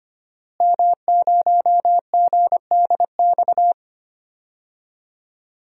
Morse code, heard as a keyed tone, about 25 wpm, 700 Hz. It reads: M0GDX
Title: DXCC-100 Call Signs - 5 Characters - Repeated 2x - 25wpm